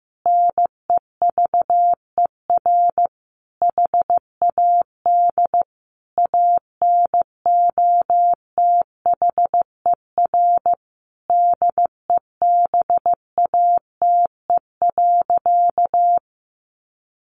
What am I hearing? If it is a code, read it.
Morse code, heard as a keyed tone, about 15 wpm, 700 Hz. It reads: NEVER HAD ANOTHER DEBATE.
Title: Sentences from Top 1100 Words 15wpm